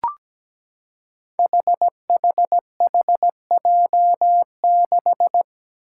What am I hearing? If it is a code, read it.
Morse code, heard as a keyed tone, about 17 wpm, 700 Hz. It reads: HHHJ6
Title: Five Challenging Letter-Number Combinations 17wpm